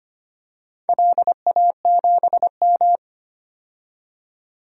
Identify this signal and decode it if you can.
Morse code, heard as a keyed tone, about 25 wpm, 700 Hz. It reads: LA7M
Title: DXCC-100 Call Signs - 4 Characters - 25wpm